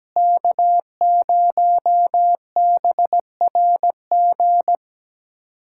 Morse code, heard as a keyed tone, about 17 wpm, 700 Hz. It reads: K0BRG
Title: USA Callsign DE Callsign - 5 Characters - 17wpm